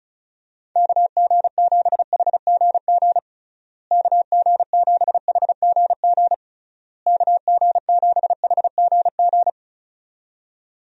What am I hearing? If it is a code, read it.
Morse code, heard as a keyed tone, about 34 wpm, 700 Hz. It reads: KG7HGG KG7HGG KG7HGG
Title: USA Call Signs - 6 Characters - Repeated 3x - 35wpm